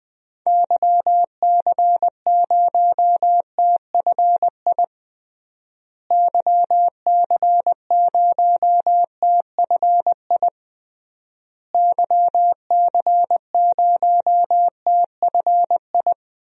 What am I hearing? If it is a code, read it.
Morse code, heard as a keyed tone, about 20 wpm, 700 Hz. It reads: YC0TFI YC0TFI YC0TFI
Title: DXCC-100 Call Signs - 6 Characters - Repeated 3x - 20wpm